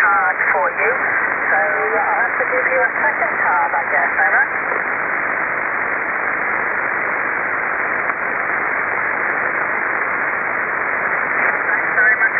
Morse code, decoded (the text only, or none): none